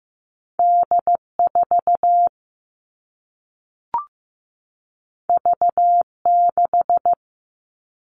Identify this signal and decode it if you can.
Morse code, heard as a keyed tone, about 15 wpm, 700 Hz. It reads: D4 V6
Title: All Cut Numbers - 2-Digit 15wpm